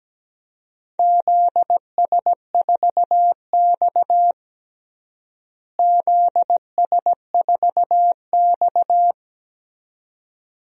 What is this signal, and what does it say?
Morse code, heard as a keyed tone, about 17 wpm, 700 Hz. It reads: ZS4X ZS4X
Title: DXCC-100 Call Signs - 4 Characters - Repeated 2x - 17wpm